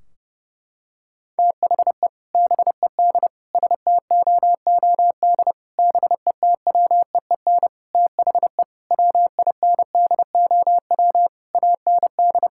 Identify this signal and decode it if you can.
Morse code, heard as a keyed tone, about 30 wpm, 700 Hz. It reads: THE BED STOOD BETWEEN THE WINDOW AND
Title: Sentences from Top 1600 Words 30wpm